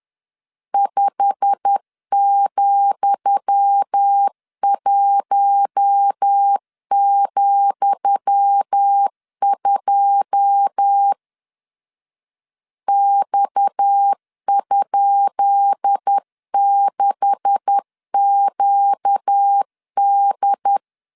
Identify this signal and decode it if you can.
Morse code, heard as a keyed tone, about 10 wpm, 800 Hz. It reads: 5,1,2 X?6QD
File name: LEZIONI / 10 SIGNS / 10D COMA 10wpm 3013